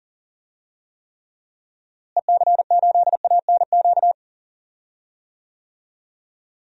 Morse code, heard as a keyed tone, about 40 wpm, 700 Hz. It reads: EC8ANQ
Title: DXCC-100 Call Signs - 6 Characters - Repeated 2x - 40wpm